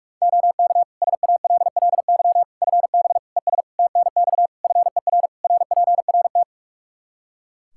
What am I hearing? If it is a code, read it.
Morse code, heard as a keyed tone, about 46 wpm, 700 Hz. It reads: OK SALLY FB ES TNX FER RPRT
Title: Practice QSO - Exchange 2 - Info - 45wpm